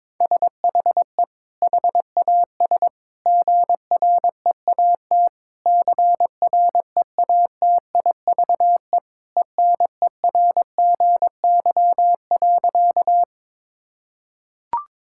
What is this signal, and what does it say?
Morse code, heard as a keyed tone, about 22 wpm, 700 Hz. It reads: SHE HAS GREAT CREATIVE ENERGY.
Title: Sentences from Top 1000 Words - Review Part 6 - 22wpm